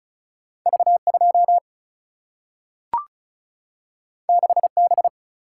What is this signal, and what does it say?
Morse code, heard as a keyed tone, about 36 wpm, 700 Hz. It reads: V2 6B
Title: Single-Letter Single-Number 35wpm